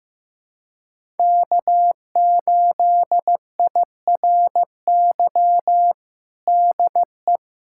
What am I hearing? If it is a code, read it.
Morse code, heard as a keyed tone, about 15 wpm, 700 Hz. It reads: K8IRY DE
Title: USA Callsign DE Callsign - 5 Characters - 15wpm